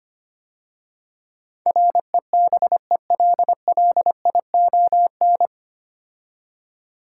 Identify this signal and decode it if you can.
Morse code, heard as a keyed tone, about 25 wpm, 700 Hz. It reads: REBELLION
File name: practice / 25wpm / 9 Letter Words - First 3 Letters Given - 25wpm